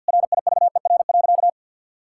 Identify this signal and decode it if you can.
Morse code, heard as a keyed tone, about 50 wpm, 700 Hz. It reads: RIVER.
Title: Sentences from Top 1000 Words - Review Part 6 - 50wpm